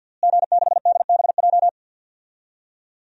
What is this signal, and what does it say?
Morse code, heard as a keyed tone, about 51 wpm, 700 Hz. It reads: G6DBJ
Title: DXCC-100 Call Signs - 5 Characters - Repeated 3x - 50wpm